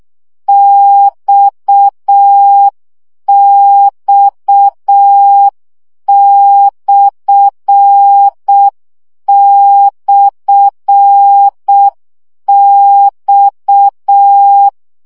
Morse code, heard as a keyed tone, about 6 wpm, 800 Hz. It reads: XX//X
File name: CW gyakorlas x per 5 baud